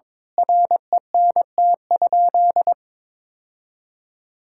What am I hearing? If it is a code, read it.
Morse code, heard as a keyed tone, about 21 wpm, 700 Hz. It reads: RENT?
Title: Sentences from Top 1000 Words - Review Part 5 - 22wpm